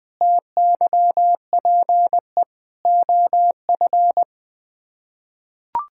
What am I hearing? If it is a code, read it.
Morse code, heard as a keyed tone, about 20 wpm, 700 Hz. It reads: TYPE OF
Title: Sets of 8 Words - 20wpm